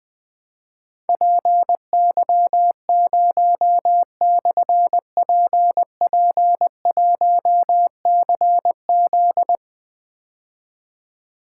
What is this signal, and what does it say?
Morse code, heard as a keyed tone, about 20 wpm, 700 Hz. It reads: PY0/PP1CZ
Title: Difficult Call Signs 20wpm